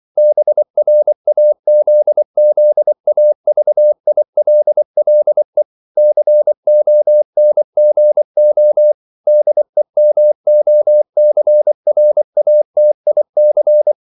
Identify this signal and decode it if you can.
Morse code, heard as a keyed tone, about 24 wpm, 600 Hz. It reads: BRAZZAVILLE CONGO DEMOCRATIC
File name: CW / Practice / NationalCapitals24